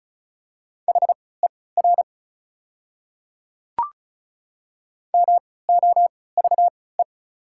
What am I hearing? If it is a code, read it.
Morse code, heard as a keyed tone, about 34 wpm, spaced as for 20 wpm, 690 Hz. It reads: HER MOVE
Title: 3x Character Spacing - Top 200 Words - 35wpm